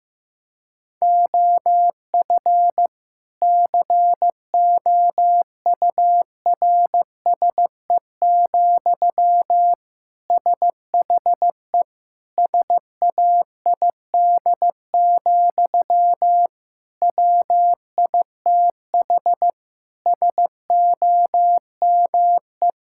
Morse code, heard as a keyed tone, about 15 wpm, 700 Hz. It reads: OF COURSE, SHE SAID, WITH SOME
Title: Sentences from Top 1600 Words 15wpm